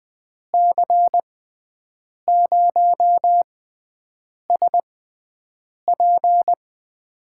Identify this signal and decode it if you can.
Morse code, heard as a keyed tone, about 20 wpm, spaced as for 7 wpm, 700 Hz. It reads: C0SP
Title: USA Call Signs - 6x Character Spacing - 20wpm